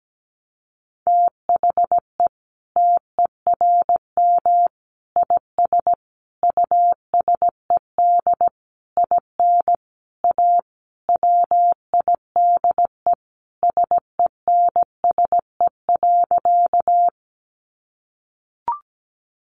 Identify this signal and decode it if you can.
Morse code, heard as a keyed tone, about 17 wpm, 700 Hz. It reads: THE TERM IS USED IN A WIDE SENSE.
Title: Sentences from Top 900 Words 17wpm